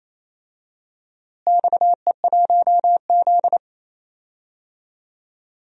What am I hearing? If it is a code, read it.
Morse code, heard as a keyed tone, about 28 wpm, 700 Hz. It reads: XE1Z
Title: DXCC-100 Call Signs - 4 Characters - Repeated 3x - 28wpm